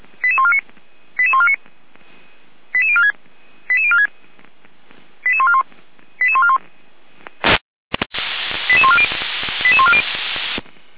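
Five-tone selective call (ZVEI), heard as quick runs of five tones, tones of about 70 ms. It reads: 88149 88149 88046 88046 88141 88141 88140 88140
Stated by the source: ZVEI five-tone selective call 88329, received by SDR.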